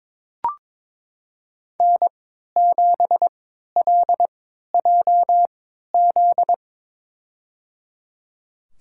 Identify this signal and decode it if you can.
Morse code, heard as a keyed tone, about 22 wpm, spaced as for 12 wpm, 700 Hz. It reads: N7LJZ
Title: USA Call Signs - 5 Characters - 3x Character Spacing - 22wpm